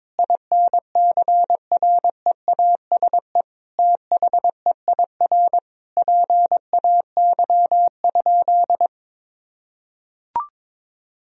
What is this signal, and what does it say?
Morse code, heard as a keyed tone, about 22 wpm, 700 Hz. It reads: INCREASE THEIR PAY?
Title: Sentences from Top 1000 Words - Review Part 4 - 22wpm